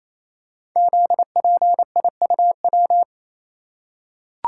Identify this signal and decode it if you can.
Morse code, heard as a keyed tone, about 27 wpm, 700 Hz. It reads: ZPIUW
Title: Code Groups - 5 Letters - 28wpm